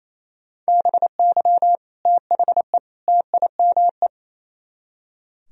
Morse code, heard as a keyed tone, about 35 wpm, 700 Hz. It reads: BY THE TIME
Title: Speed-Racing - Sets of 3 Words 28wpm